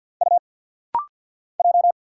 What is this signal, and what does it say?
Morse code, heard as a keyed tone, about 50 wpm, 700 Hz. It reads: U J
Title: Single Letter-Number - Rapid-Fire 50wpm